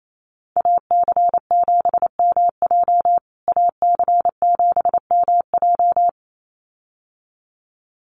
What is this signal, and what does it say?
Morse code, heard as a keyed tone, about 28 wpm, 700 Hz. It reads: AC7MJ AC7MJ
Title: USA Call Signs Repeated 2x 28wpm